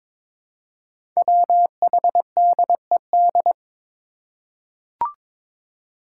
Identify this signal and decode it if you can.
Morse code, heard as a keyed tone, about 22 wpm, 700 Hz. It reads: WHDED
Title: Five Challenging Letter-Number Combinations 22wpm